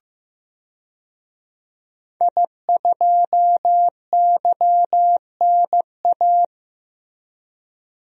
Morse code, heard as a keyed tone, about 15 wpm, 700 Hz. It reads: I2YNA
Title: DXCC-100 Call Signs - 5 Characters - Repeated 2x - 15wpm